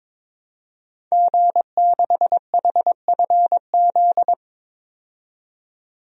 Morse code, heard as a keyed tone, about 22 wpm, 700 Hz. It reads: G6HFZ
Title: DXCC-100 Call Signs - 5 Characters - 22wpm